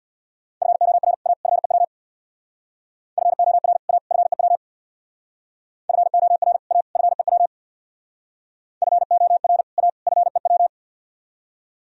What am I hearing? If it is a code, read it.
Morse code, heard as a keyed tone, about 60 wpm, 700 Hz. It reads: FOR A FEW FOR A FEW FOR A FEW FOR A FEW
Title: Speed-Racing - Sets of 3 Words 50wpm